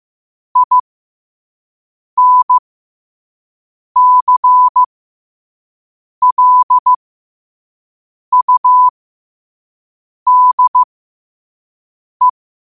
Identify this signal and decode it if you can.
Morse code, heard as a keyed tone, about 15 wpm, spaced as for 5 wpm, 1000 Hz. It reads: INCLUDE